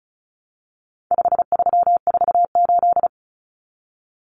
Morse code, heard as a keyed tone, about 34 wpm, 700 Hz. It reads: 5348
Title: Common Cut Numbers - 4-Digit 35wpm